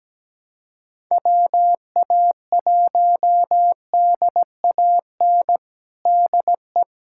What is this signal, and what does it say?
Morse code, heard as a keyed tone, about 17 wpm, 700 Hz. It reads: WA1DAN DE
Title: USA Callsign DE Callsign 17wpm